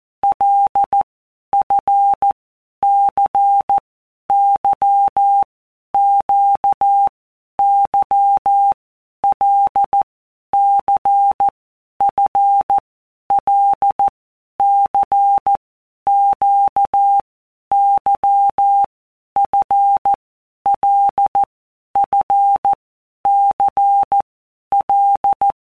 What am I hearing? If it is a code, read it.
Morse code, heard as a keyed tone, about 14 wpm, 800 Hz. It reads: L F C Y Q Y L C F L C Q Y F L F C L
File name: Basic Letter Groups By WPM's / Groups@13 / LFCYQ@13